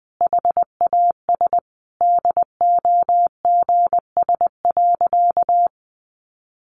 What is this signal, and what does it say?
Morse code, heard as a keyed tone, about 20 wpm, 700 Hz. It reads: HAS DOGS.
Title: Additional Easy Phrases 20wpm